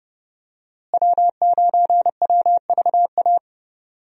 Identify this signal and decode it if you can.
Morse code, heard as a keyed tone, about 30 wpm, 700 Hz. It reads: W9WVA
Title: USA Call Signs - 5 Characters - Repeated 2x - 30wpm